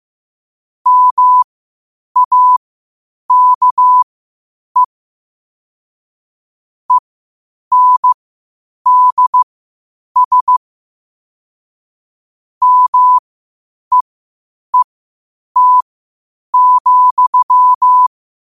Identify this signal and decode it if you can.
Morse code, heard as a keyed tone, about 15 wpm, spaced as for 8 wpm, 1000 Hz. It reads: MAKE ENDS MEET,
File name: qotdfiles / 8wpm